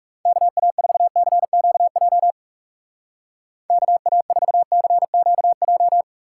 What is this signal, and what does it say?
Morse code, heard as a keyed tone, about 36 wpm, 700 Hz. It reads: KA4CQJ KA4CQJ
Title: Speed Racing - USA 2x3 Call Signs 30wpm